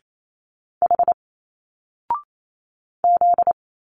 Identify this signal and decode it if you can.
Morse code, heard as a keyed tone, about 28 wpm, 700 Hz. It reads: H Z
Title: Single Letters 28wpm